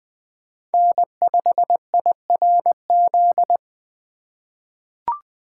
Morse code, heard as a keyed tone, about 20 wpm, 700 Hz. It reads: N5IRZ
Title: USA Call Signs - 5 Characters - 4x Character Spacing - 20wpm